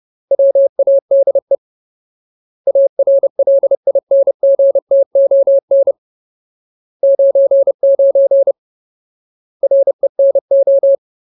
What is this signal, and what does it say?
Morse code, heard as a keyed tone, about 30 wpm, 550 Hz. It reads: WADE ARLINGTON 99 RENO
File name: Ultimate Word List / 30WPM Ultimate